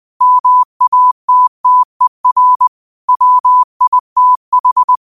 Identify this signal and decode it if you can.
Morse code, heard as a keyed tone, about 20 wpm, 1000 Hz. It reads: MATTER WITH
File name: qotdfiles / 20wpm